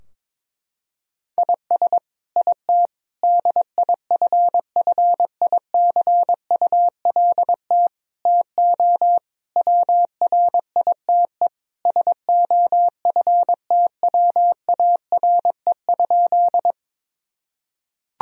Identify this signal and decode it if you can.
Morse code, heard as a keyed tone, about 22 wpm, 700 Hz. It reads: IS IT DIFFICULT TO WRITE SOFTWARE?
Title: Sentences from Top 1000 Words - Review Part 3 - 22wpm